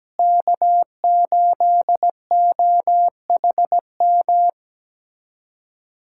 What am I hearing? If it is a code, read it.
Morse code, heard as a keyed tone, about 17 wpm, 700 Hz. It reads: K8OHM
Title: USA Callsign DE Callsign - 5 Characters - 17wpm